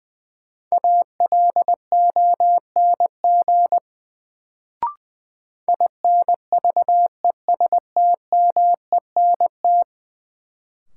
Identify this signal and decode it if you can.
Morse code, heard as a keyed tone, about 20 wpm, 700 Hz. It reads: ALONG INVESTMENT
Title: Sentences from Top 700 Words - New Word Drill - 20wpm